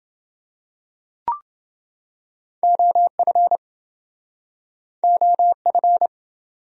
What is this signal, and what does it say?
Morse code, heard as a keyed tone, about 28 wpm, 700 Hz. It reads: OF OF